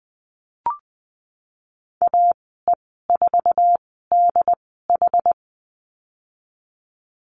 Morse code, heard as a keyed tone, about 20 wpm, spaced as for 15 wpm, 700 Hz. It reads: AE4DH
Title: 076 Lesson 13g - Letter L - Call Signs 20wpm